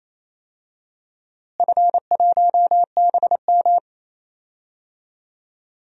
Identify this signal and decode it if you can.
Morse code, heard as a keyed tone, about 28 wpm, 700 Hz. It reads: F1BM
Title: DXCC-100 Call Signs - 4 Characters - Repeated 3x - 28wpm